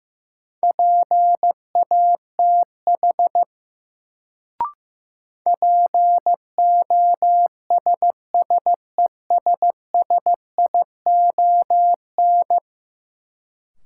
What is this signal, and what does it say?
Morse code, heard as a keyed tone, about 15 wpm, 700 Hz. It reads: PATH POSSESSION
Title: Sentences from Top 1400 Words - New Word Drill - 15wpm